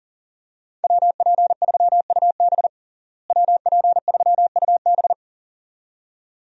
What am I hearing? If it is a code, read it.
Morse code, heard as a keyed tone, about 40 wpm, 700 Hz. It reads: WP3UB WP3UB
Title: USA Call Signs - 5 Characters - Repeated 2x - 40wpm